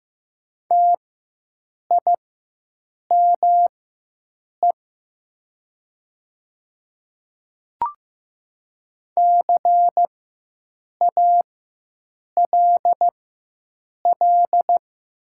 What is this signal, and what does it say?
Morse code, heard as a keyed tone, about 15 wpm, spaced as for 7 wpm, 700 Hz. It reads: TIME CALL
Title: 4x Character Spacing - Top 200 Words - 15wpm